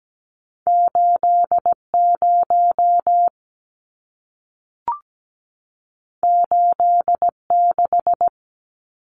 Morse code, heard as a keyed tone, about 17 wpm, 700 Hz. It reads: 80 86
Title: Numbers - 2-Digit 17wpm